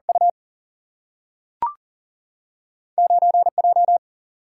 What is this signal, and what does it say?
Morse code, heard as a keyed tone, about 40 wpm, 700 Hz. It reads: U 9J
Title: Single-Letter Single-Number 40wpm